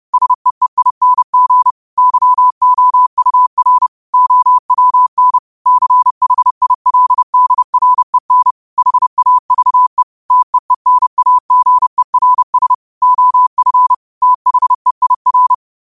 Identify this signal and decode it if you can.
Morse code, heard as a keyed tone, about 30 wpm, 1000 Hz. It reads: SEEING YOUR OWN CHILDREN HAVE TEENAGERS OF THEIR